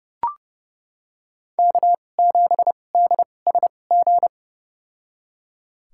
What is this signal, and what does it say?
Morse code, heard as a keyed tone, about 30 wpm, spaced as for 22 wpm, 700 Hz. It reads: K7DSG
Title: USA Call Signs - 2x Character Spacing - 30wpm